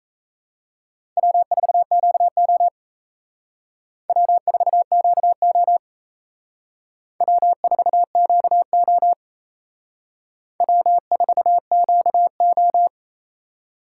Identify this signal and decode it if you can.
Morse code, heard as a keyed tone, about 35 wpm, 700 Hz. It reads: W4QO W4QO W4QO W4QO
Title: Speed Racing - USA 1x2 Call Signs 28wpm